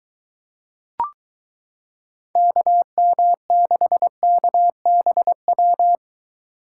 Morse code, heard as a keyed tone, about 22 wpm, 700 Hz. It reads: KM6KBW